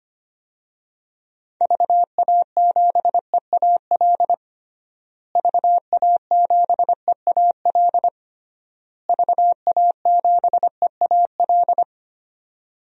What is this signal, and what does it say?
Morse code, heard as a keyed tone, about 25 wpm, 700 Hz. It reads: VA7EAL VA7EAL VA7EAL
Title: DXCC-100 Call Signs - 6 Characters - Repeated 3x - 25wpm